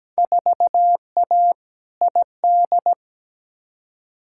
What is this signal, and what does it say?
Morse code, heard as a keyed tone, about 17 wpm, 700 Hz. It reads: 4A ID
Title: Field Day Exchanges Encore - 17wpm